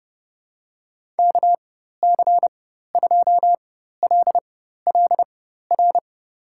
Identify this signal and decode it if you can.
Morse code, heard as a keyed tone, about 30 wpm, spaced as for 14 wpm, 700 Hz. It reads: KC2LLR
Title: USA Call Signs - 6 Characters - 4x Character Spacing - 30wpm